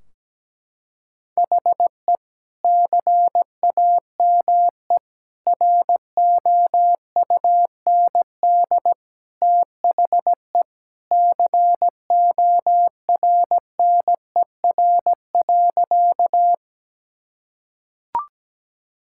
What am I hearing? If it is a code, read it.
Morse code, heard as a keyed tone, about 17 wpm, 700 Hz. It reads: HE CAME ROUND THE CORNER.
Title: Sentences from Top 1600 Words 17wpm